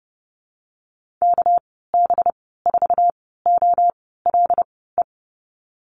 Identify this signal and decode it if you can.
Morse code, heard as a keyed tone, about 30 wpm, spaced as for 17 wpm, 700 Hz. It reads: KB4OLE
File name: Lesson 37g - Letter x - Call Signs 30wpm